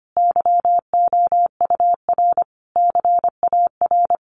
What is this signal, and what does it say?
Morse code, heard as a keyed tone, about 25 wpm, 700 Hz. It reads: YOUR CAR
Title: Sentences from Top 1100 Words 25wpm